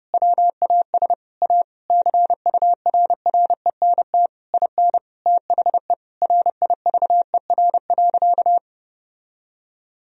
Morse code, heard as a keyed tone, about 31 wpm, 700 Hz. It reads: WAS A CURRENT IN THE RIVER.
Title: Sentences from Top 1000 Words - Review Part 4 - 30wpm